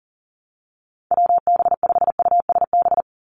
Morse code, heard as a keyed tone, about 40 wpm, 700 Hz. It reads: WB5USB
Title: NA SOTA Activator Callsigns 40wpm